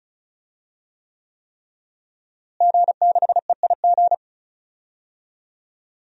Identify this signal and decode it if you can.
Morse code, heard as a keyed tone, about 36 wpm, 700 Hz. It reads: G6EIG
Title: DXCC-100 Call Signs - 5 Characters - Repeated 3x - 35wpm